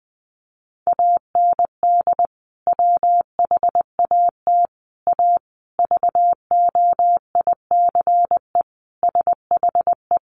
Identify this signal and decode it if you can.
Morse code, heard as a keyed tone, about 20 wpm, 700 Hz. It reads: AND WHAT A VOICE SHE
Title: Sentences from Top 500 Words 20wpm